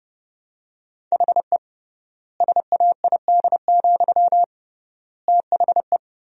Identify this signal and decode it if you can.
Morse code, heard as a keyed tone, about 32 wpm, 700 Hz. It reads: HE SAID, THE
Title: 3x Word-Spacing - Sentences from Top 400 Words - 30wpm